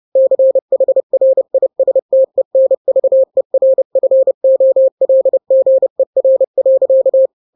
Morse code, heard as a keyed tone, about 30 wpm, 530 Hz. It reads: CHRISTENVERFOLGER.
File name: 29wpm